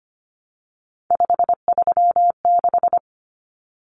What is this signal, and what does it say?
Morse code, heard as a keyed tone, about 25 wpm, 700 Hz. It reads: 536
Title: Numbers - 3-Digit 25wpm